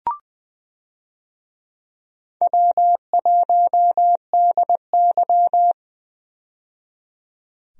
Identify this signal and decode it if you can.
Morse code, heard as a keyed tone, about 20 wpm, 700 Hz. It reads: W1DY
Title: USA Call Signs - 4 Characters - 20wpm